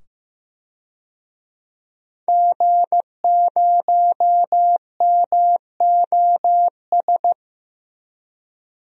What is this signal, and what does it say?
Morse code, heard as a keyed tone, about 15 wpm, 700 Hz. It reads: G0MOS
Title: DXCC-100 Call Signs - 5 Characters - 15wpm